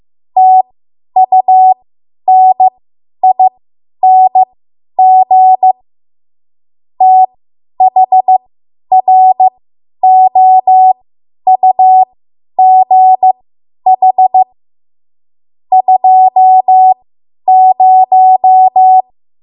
Morse code, heard as a keyed tone, about 15 wpm, spaced as for 10 wpm, 750 Hz. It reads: TUNING THROUGH 20
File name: Morse / 10WPM